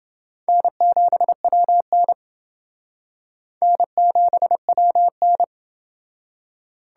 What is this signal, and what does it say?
Morse code, heard as a keyed tone, about 28 wpm, 700 Hz. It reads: N7WN N7WN